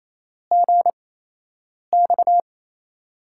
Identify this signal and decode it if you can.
Morse code, heard as a keyed tone, about 27 wpm, spaced as for 8 wpm, 700 Hz. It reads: GX
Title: USA Call Signs - 6 Characters - 8x Character Spacing - 28wpm